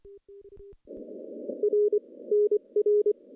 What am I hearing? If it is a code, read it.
Morse code, heard as a keyed tone, about 22 wpm, 420 Hz. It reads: R NR
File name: contests / CW-SS-k5tr- -0200